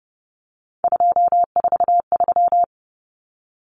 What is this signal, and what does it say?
Morse code, heard as a keyed tone, about 30 wpm, 700 Hz. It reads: 243
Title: Numbers - 3-Digit 30wpm